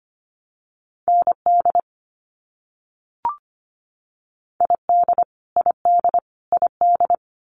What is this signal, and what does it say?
Morse code, heard as a keyed tone, about 25 wpm, 700 Hz. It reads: ND ID ID ID
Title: US States Repeated 3x 25wpm